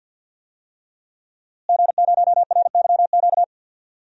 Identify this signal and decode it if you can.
Morse code, heard as a keyed tone, about 47 wpm, 700 Hz. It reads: G0RYQ
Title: DXCC-100 Call Signs - 5 Characters - Repeated 3x - 50wpm